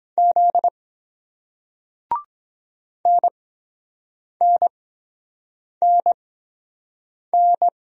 Morse code, heard as a keyed tone, about 23 wpm, spaced as for 7 wpm, 700 Hz. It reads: Z NNNN